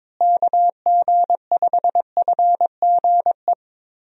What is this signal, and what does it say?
Morse code, heard as a keyed tone, about 22 wpm, 700 Hz. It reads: KG5FGE
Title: USA Call Signs - 6 Characters - Repeated 2x - 22wpm